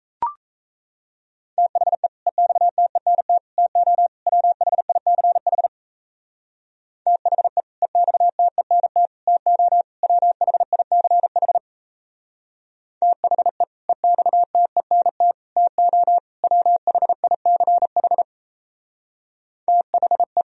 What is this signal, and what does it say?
Morse code, heard as a keyed tone, about 35 wpm, 700 Hz. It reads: THE EXTENT TO WHICH THE EXTENT TO WHICH THE EXTENT TO WHICH THE
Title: Speed-Racing - Sets of 4 Words 28wpm